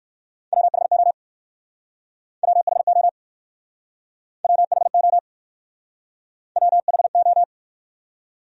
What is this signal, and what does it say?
Morse code, heard as a keyed tone, about 60 wpm, 700 Hz. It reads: WHO WHO WHO WHO